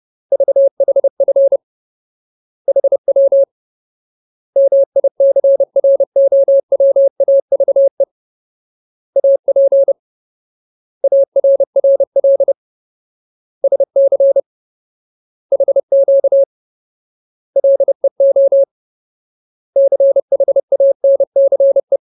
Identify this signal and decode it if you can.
Morse code, heard as a keyed tone, about 30 wpm, 550 Hz. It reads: VHF HW MICROWAVE AP ARRL SC HQ LEO CHANCE
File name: Ultimate Word List / 30WPM Ultimate